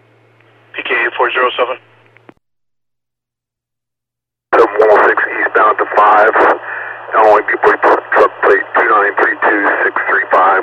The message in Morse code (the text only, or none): none